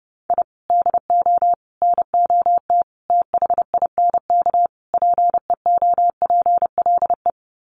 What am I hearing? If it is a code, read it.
Morse code, heard as a keyed tone, about 30 wpm, 700 Hz. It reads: I DO NOT THINK PEOPLE
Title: Sets of 5 Words Encore2 30wpm